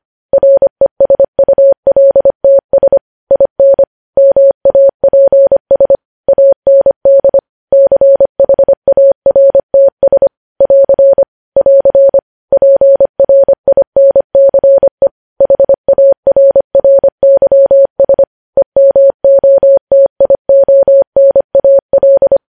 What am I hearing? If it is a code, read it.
Morse code, heard as a keyed tone, about 25 wpm, 550 Hz. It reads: RESULTS IN MAPS AND CHARTS <AR> <AR> PRINCE HARRYS EMOTIONAL
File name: podcast / News headlines-1x-25x25